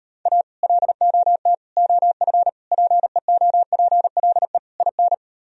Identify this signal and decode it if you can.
Morse code, heard as a keyed tone, about 38 wpm, 700 Hz. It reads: A LOT OF PEOPLE IN